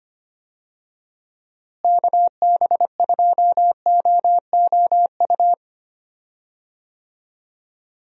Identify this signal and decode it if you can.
Morse code, heard as a keyed tone, about 25 wpm, 700 Hz. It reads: KB2OOU
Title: USA Call Signs - 6 Characters - Repeated 2x - 25wpm